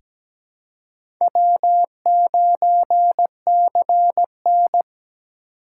Morse code, heard as a keyed tone, about 17 wpm, 700 Hz. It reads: W9CN
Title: USA Call Signs - 4 Characters - 6x Character Spacing - 17wpm